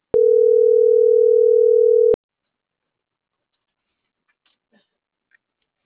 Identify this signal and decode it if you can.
Ringback tone (North America), 440 + 480 Hz.